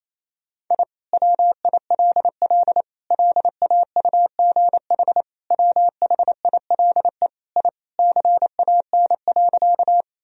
Morse code, heard as a keyed tone, about 28 wpm, 700 Hz. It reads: I WILL LAUGH WHILE I CAN.
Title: Sentences from Top 1000 Words - Review Part 1 - 28wpm